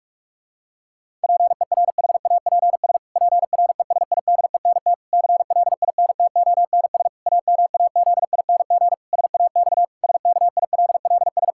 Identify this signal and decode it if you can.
Morse code, heard as a keyed tone, about 46 wpm, 700 Hz. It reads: PERHAPS PRESIDENT CLINTONS AMAZING SAX SKILLS
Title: Pangram Challenge v2 - 45wpm